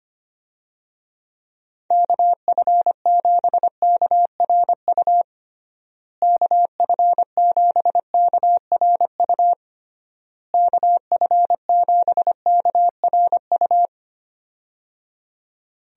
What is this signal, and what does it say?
Morse code, heard as a keyed tone, about 25 wpm, 700 Hz. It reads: KF7KRU KF7KRU KF7KRU
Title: USA Call Signs - 6 Characters - Repeated 3x - 25wpm